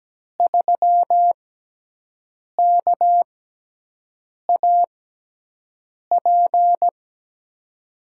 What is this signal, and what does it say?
Morse code, heard as a keyed tone, about 17 wpm, spaced as for 6 wpm, 700 Hz. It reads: 3KAP
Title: USA Call Signs - 6x Character Spacing - 17wpm